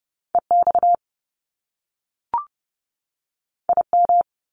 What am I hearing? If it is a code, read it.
Morse code, heard as a keyed tone, about 30 wpm, 700 Hz. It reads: E X I M
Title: Top Two Letter Combinations in Words 30wpm